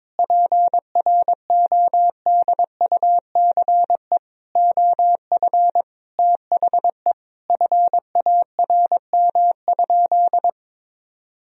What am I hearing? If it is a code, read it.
Morse code, heard as a keyed tone, about 22 wpm, 700 Hz. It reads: PRODUCE OF THE FARM?
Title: Sentences from Top 1000 Words - Review Part 3 - 22wpm